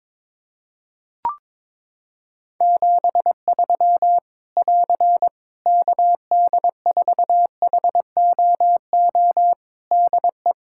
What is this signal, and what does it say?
Morse code, heard as a keyed tone, about 22 wpm, 700 Hz. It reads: 73 <AR> KD4HOO DE
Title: Practice QSO - Exchange 4 - Ending - 22wpm